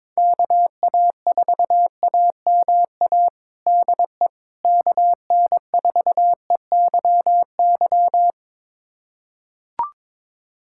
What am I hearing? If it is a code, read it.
Morse code, heard as a keyed tone, about 21 wpm, 700 Hz. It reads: KA4AMA DE KN4EYY
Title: USA Callsign DE Callsign 22wpm